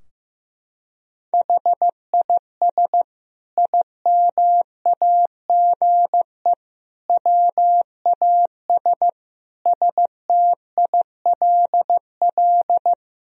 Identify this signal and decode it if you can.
Morse code, heard as a keyed tone, about 15 wpm, 700 Hz. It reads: HIS IMAGE WAS STILL
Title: Sentences from Top 1000 Words - Review Part 4 - 15wpm